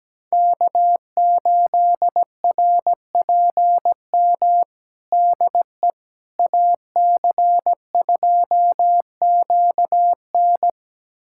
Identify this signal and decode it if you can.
Morse code, heard as a keyed tone, about 17 wpm, 700 Hz. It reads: K8RPM DE AC2QN
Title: USA Callsign DE Callsign - 5 Characters - 17wpm